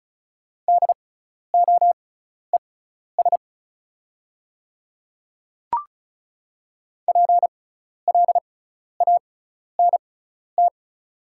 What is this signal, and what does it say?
Morse code, heard as a keyed tone, about 36 wpm, spaced as for 12 wpm, 700 Hz. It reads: DOES PLANT
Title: 6x Character Spacing - Top 300 Words - 35wpm